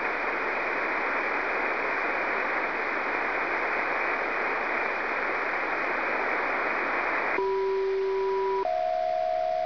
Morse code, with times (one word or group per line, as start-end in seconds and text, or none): none